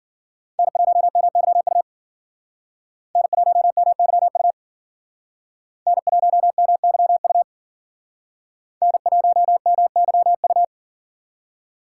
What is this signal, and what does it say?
Morse code, heard as a keyed tone, about 51 wpm, 700 Hz. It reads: N1MYU N1MYU N1MYU N1MYU